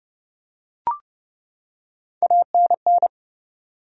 Morse code, heard as a keyed tone, about 30 wpm, 700 Hz. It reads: ANN
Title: USA Common Names - Max 3 Letters - 30wpm